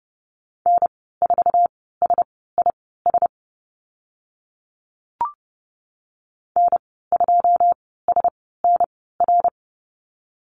Morse code, heard as a keyed tone, about 30 wpm, spaced as for 17 wpm, 700 Hz. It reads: N4SIS N2SNR
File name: Lesson 15g - Number 5 - Call Signs 30wpm